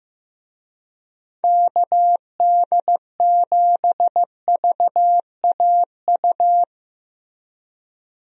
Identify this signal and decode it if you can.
Morse code, heard as a keyed tone, about 15 wpm, 700 Hz. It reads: KD7VAU
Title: USA Call Signs - 6 Characters - Repeated 2x - 15wpm